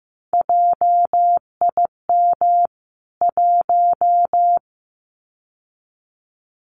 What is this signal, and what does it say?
Morse code, heard as a keyed tone, about 15 wpm, 700 Hz. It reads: JIM 1
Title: CWT Exchange 15wpm